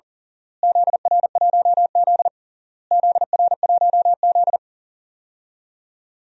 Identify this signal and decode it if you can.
Morse code, heard as a keyed tone, about 40 wpm, 700 Hz. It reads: ZR1Z ZR1Z
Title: DXCC-100 Call Signs - 4 Characters - Repeated 3x - 40wpm